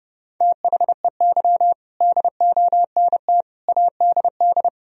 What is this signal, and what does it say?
Morse code, heard as a keyed tone, about 30 wpm, 700 Hz. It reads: THEY DONT ADD
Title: Sentences from Top 1000 Words - Review Part 1 - 30wpm